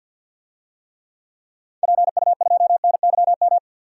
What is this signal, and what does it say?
Morse code, heard as a keyed tone, about 46 wpm, 700 Hz. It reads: PU2NYM
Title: DXCC-100 Call Signs - 6 Characters - 50wpm